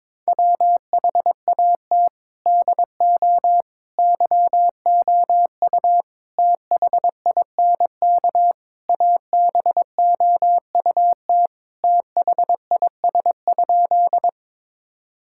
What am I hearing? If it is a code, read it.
Morse code, heard as a keyed tone, about 21 wpm, 700 Hz. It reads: WHAT DO YOU THINK ABOUT THIS?
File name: practice / 22wpm / Sentences with 4-Word Phrases - 22wpm